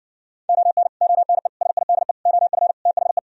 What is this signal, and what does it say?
Morse code, heard as a keyed tone, about 70 wpm, 700 Hz. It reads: ON ONE SIDE OF THE